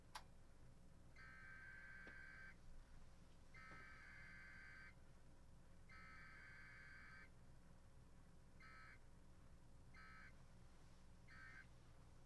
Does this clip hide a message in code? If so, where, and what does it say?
Yes: Emergency Alert System (SAME) tones (header bursts) from 1.1 to 11.7 s, heard as harsh buzzing bursts.